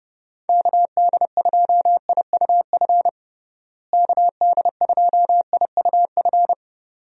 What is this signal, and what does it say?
Morse code, heard as a keyed tone, about 30 wpm, 700 Hz. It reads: KD2IUF KD2IUF
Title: USA Call Signs - 6 Characters - Repeated 2x - 30wpm